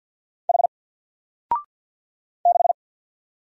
Morse code, heard as a keyed tone, about 50 wpm, 700 Hz. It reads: H 6
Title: Single Character (Letter:Number:Punctuation:Prosign) 50wpm